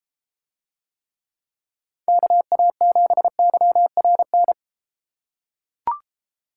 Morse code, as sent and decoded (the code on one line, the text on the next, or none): -.- .- --... -.-- .-. -.
KA7YRN